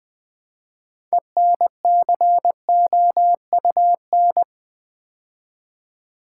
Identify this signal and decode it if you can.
Morse code, heard as a keyed tone, about 20 wpm, 700 Hz. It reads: ENCOUN
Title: Build A Word - 3 Letters At A Time - 9-Letter Words - 20wpm